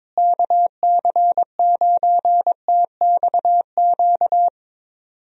Morse code, heard as a keyed tone, about 21 wpm, 700 Hz. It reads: KC9TXQ
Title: USA Call Signs - 6 Characters - Repeated 2x - 22wpm